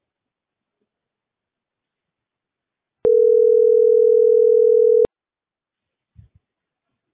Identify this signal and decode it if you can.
Ringback tone (North America), 440 + 480 Hz.